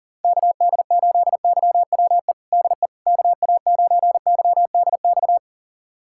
Morse code, heard as a keyed tone, about 40 wpm, 700 Hz. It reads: KD8YWE DE KA9YDX
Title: USA Callsign DE Callsign 40wpm